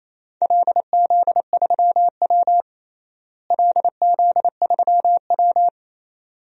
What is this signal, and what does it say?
Morse code, heard as a keyed tone, about 28 wpm, 700 Hz. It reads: LZ3W LZ3W
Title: DXCC-100 Call Signs - 4 Characters - Repeated 2x - 28wpm